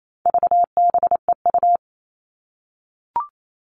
Morse code, heard as a keyed tone, about 28 wpm, 700 Hz. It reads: VBEU
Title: All Cut Numbers - 4-Digit 28wpm